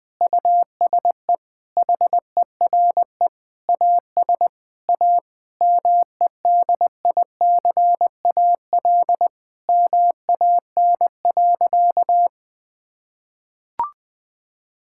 Morse code, heard as a keyed tone, about 21 wpm, 700 Hz. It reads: USE HERE AS A MEDICAL MAN.
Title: Sentences from Top 1000 Words - Review Part 5 - 20wpm